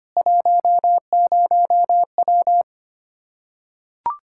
Morse code, heard as a keyed tone, about 25 wpm, 700 Hz. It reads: 10W
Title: Practice QSO - Exchange 2 - Info - 25wpm